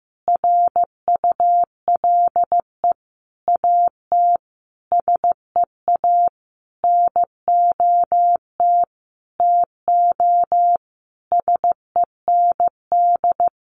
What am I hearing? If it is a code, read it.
Morse code, heard as a keyed tone, about 15 wpm, 700 Hz. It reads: RULE AT SEA NOT TO SEND
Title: Sentences from Top 1500 Words 15wpm